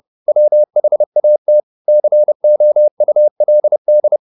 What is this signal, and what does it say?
Morse code, heard as a keyed tone, about 30 wpm, 600 Hz. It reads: WHAT COULD